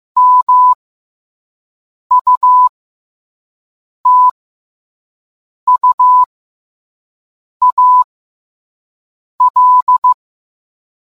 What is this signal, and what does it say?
Morse code, heard as a keyed tone, about 15 wpm, spaced as for 5 wpm, 1000 Hz. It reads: MUTUAL